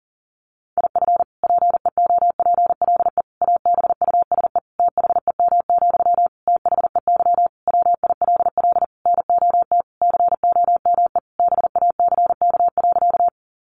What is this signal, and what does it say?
Morse code, heard as a keyed tone, about 40 wpm, 700 Hz. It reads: IF PEOPLE ABUSE THEM, THEY WILL NOT COME BACK.
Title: Sentences from Top 1300 Words 40wpm